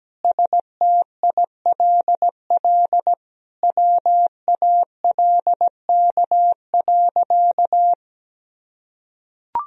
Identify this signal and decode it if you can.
Morse code, heard as a keyed tone, about 17 wpm, 700 Hz. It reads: STILL WALK.
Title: Sentences from Top 1000 Words - Review Part 1 - 17wpm